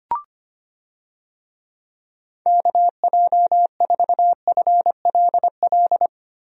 Morse code, heard as a keyed tone, about 25 wpm, 700 Hz. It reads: KJ4FLL
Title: USA Call Signs - 6 Characters - Repeated 3x - 25wpm